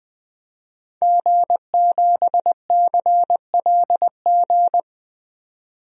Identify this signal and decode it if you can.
Morse code, heard as a keyed tone, about 20 wpm, 700 Hz. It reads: G7CLG
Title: DXCC-100 Call Signs - 5 Characters - 20wpm